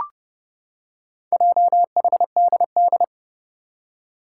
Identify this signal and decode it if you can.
Morse code, heard as a keyed tone, about 30 wpm, 700 Hz. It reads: JHDD
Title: Four Challenging Letter-Number Combinations 30wpm